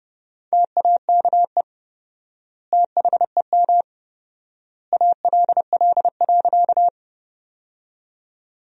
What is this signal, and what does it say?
Morse code, heard as a keyed tone, about 30 wpm, 700 Hz. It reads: TAKE THEM ALL.
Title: 4x Word-Spacing - Sentences from Top 200 Words - 30wpm